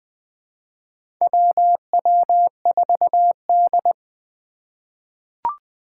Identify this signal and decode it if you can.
Morse code, heard as a keyed tone, about 20 wpm, 700 Hz. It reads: WW4D
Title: 2023 SOTA Chaser Call Signs - 20wpm